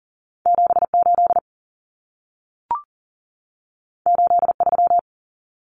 Morse code, heard as a keyed tone, about 40 wpm, 700 Hz. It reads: 78 83
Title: Numbers - 2-Digit 40wpm